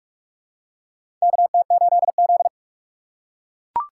Morse code, heard as a keyed tone, about 43 wpm, 700 Hz. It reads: KT8Z
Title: USA Call Signs - 4 Characters - 3x Character Spacing - 45wpm